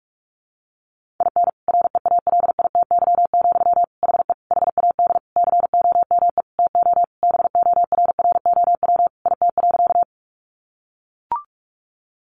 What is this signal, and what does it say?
Morse code, heard as a keyed tone, about 44 wpm, 700 Hz. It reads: IN REALITY, HE HAD COME TO BORROW IT.
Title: Sentences from Top 1500 Words 45wpm